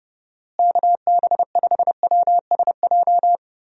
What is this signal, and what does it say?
Morse code, heard as a keyed tone, about 30 wpm, 700 Hz. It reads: KB5WSJ
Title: USA Call Signs - 6 Characters - Repeated 3x - 30wpm